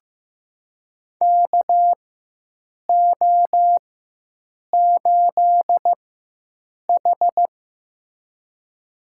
Morse code, heard as a keyed tone, about 15 wpm, spaced as for 7 wpm, 700 Hz. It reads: KO8H
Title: USA Call Signs - 4x Character Spacing - 15wpm